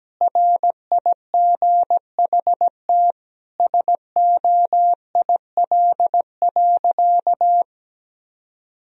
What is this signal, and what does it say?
Morse code, heard as a keyed tone, about 17 wpm, 700 Hz. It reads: RIGHT SOIL.
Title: Sentences from Top 1000 Words - Review Part 4 - 17wpm